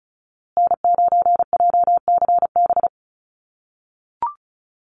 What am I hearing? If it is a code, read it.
Morse code, heard as a keyed tone, about 34 wpm, 700 Hz. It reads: N9JCB
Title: USA Call Signs Repeated 2x 35wpm